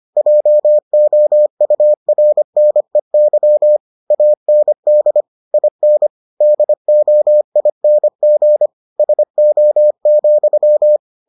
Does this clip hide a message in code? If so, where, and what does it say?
Yes: Morse code from 0.2 to 11.0 s, heard as a keyed tone, about 25 wpm, 600 Hz. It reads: JOURNEY AND IN DOING SO,